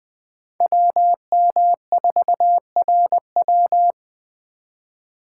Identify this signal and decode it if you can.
Morse code, heard as a keyed tone, about 21 wpm, 700 Hz. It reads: WM4RW
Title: Speed Racing - USA 2x2 Call Signs 17wpm